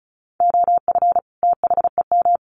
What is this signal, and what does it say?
Morse code, heard as a keyed tone, about 34 wpm, 700 Hz. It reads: OF THEM
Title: Sets of 2 Words 35wpm